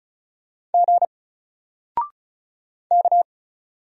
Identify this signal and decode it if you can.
Morse code, heard as a keyed tone, about 36 wpm, 700 Hz. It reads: G K
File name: Single-Letter-Number-35wpm